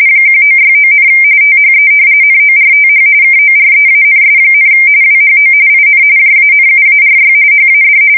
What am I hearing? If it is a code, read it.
RTTY, about 45.45 baud. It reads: 1969, THE NORTH KOREANS, WHO IN JANUARY HAD
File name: fsk170 / battleshipnj